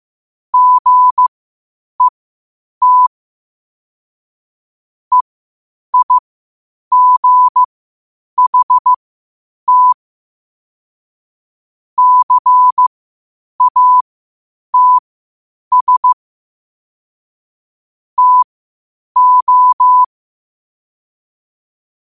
Morse code, heard as a keyed tone, about 15 wpm, spaced as for 8 wpm, 1000 Hz. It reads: GET EIGHT CATS TO
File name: qotdfiles / 8wpm